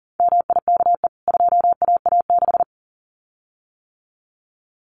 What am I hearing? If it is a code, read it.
Morse code, heard as a keyed tone, about 40 wpm, 700 Hz. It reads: MIKE 2AA6
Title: CWT Exchange 40wpm